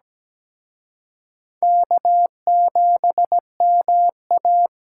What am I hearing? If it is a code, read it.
Morse code, heard as a keyed tone, about 20 wpm, 700 Hz. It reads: K7MA
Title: Speed Racing - USA Call Signs 17wpm